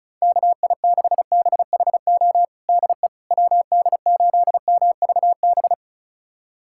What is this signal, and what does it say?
Morse code, heard as a keyed tone, about 34 wpm, 700 Hz. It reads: KI6BHO DE WD8MVB
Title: USA Callsign DE Callsign 35wpm